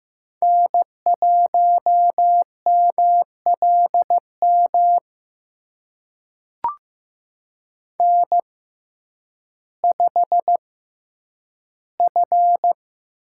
Morse code, heard as a keyed tone, about 15 wpm, spaced as for 9 wpm, 700 Hz. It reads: N1MLM N5F
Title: USA Call Signs - 6x Character Spacing - 15wpm